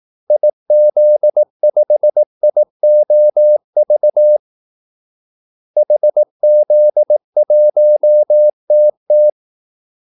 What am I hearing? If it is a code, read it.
Morse code, heard as a keyed tone, about 18 wpm, 600 Hz. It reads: IZ5IOV HZ1TT
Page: Kaldesignaler 18 wpm